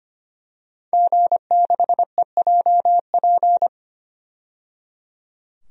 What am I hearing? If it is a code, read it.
Morse code, heard as a keyed tone, about 25 wpm, 700 Hz. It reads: G6EJP
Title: DXCC-100 Call Signs - 5 Characters - 25wpm